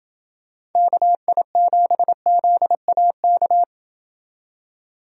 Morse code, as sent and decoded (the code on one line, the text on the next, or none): -.- .. --... --.. .- -.-
KI7ZAK